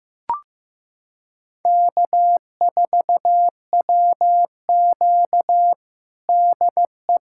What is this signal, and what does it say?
Morse code, heard as a keyed tone, about 15 wpm, 700 Hz. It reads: K4WQ DE
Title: USA Callsign DE Callsign 15wpm